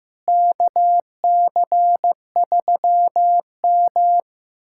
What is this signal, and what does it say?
Morse code, heard as a keyed tone, about 15 wpm, 700 Hz. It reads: KC3M
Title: USA Callsign DE Callsign - 4 Characters - 15wpm